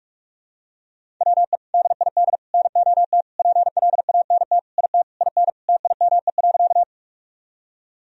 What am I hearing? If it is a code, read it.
Morse code, heard as a keyed tone, about 45 wpm, 700 Hz. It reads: WE DID NOT PLANT IT IN TIME.
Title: Sentences from Top 1000 Words - Review Part 1 - 45wpm